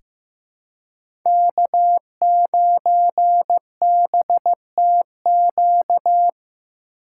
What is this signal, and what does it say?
Morse code, heard as a keyed tone, about 15 wpm, 700 Hz. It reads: K9BTQ
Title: K1USN SST QSO Practice - 15wpm